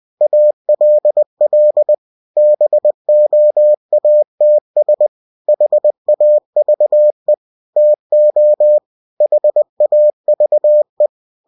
Morse code, heard as a keyed tone, about 20 wpm, 600 Hz. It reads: ALL BOATS HAVE TO HAVE
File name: cw-books / winnie-the-pooh / winnie-20wpm-page-0095